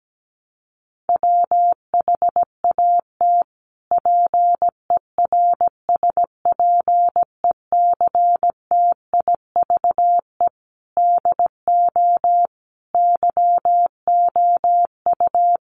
Morse code, heard as a keyed tone, about 17 wpm, 700 Hz. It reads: WHAT PERSPECTIVE DO YOU
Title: Sentences from Top 1200 Words 17wpm